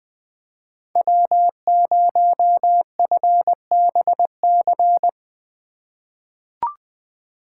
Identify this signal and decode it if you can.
Morse code, heard as a keyed tone, about 21 wpm, 700 Hz. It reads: W0FBC
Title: USA Call Signs - 5 Characters - 4x Character Spacing - 20wpm